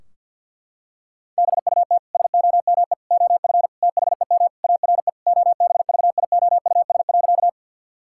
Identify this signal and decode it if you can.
Morse code, heard as a keyed tone, about 51 wpm, 700 Hz. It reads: BUT SOME OF THEM ARE OBVIOUS.
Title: Sentences from Top 1600 Words 50wpm